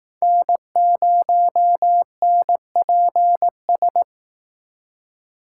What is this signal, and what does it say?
Morse code, heard as a keyed tone, about 18 wpm, 700 Hz. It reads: N0NPS
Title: Speed Racing - USA Call Signs 15wpm